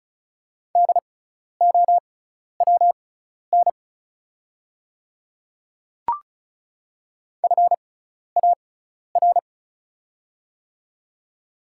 Morse code, heard as a keyed tone, about 34 wpm, spaced as for 12 wpm, 690 Hz. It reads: DOWN FAR
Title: 6x Character Spacing - Top 500 Words - 35wpm